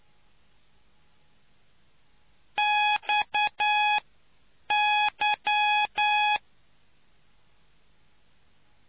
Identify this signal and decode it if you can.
Morse code, heard as a keyed tone, about 9 wpm, spaced as for 7 wpm, 840 Hz. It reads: XY